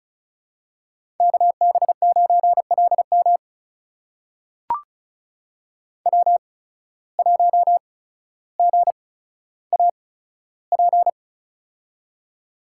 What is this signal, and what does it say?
Morse code, heard as a keyed tone, about 36 wpm, spaced as for 15 wpm, 700 Hz. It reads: KB9LM W1GAP
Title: USA Call Signs - 5 Characters - 8x Character Spacing - 35wpm